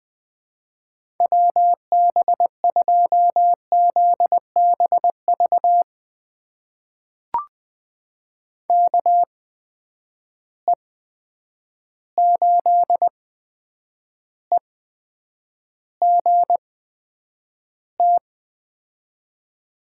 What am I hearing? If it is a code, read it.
Morse code, heard as a keyed tone, about 21 wpm, spaced as for 9 wpm, 700 Hz. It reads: WB2ZBV KE8EGT
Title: USA Call Signs - 8x Character Spacing - 20wpm